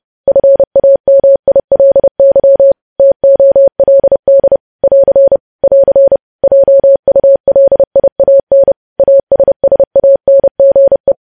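Morse code, heard as a keyed tone, about 30 wpm, 550 Hz. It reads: FAMILY TOLD <AR> <AR> JULIAN ASSANGE
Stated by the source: Morse news headlines sent at 30.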